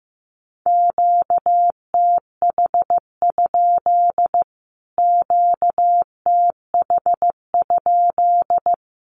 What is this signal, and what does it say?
Morse code, heard as a keyed tone, about 15 wpm, 700 Hz. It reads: QTH? QTH?
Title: Q-Codes Repeated 3x 15wpm